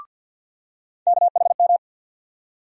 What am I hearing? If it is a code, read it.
Morse code, heard as a keyed tone, about 52 wpm, 690 Hz. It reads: KHM
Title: USA Call Sign Postfixes - 3 Letters 50wpm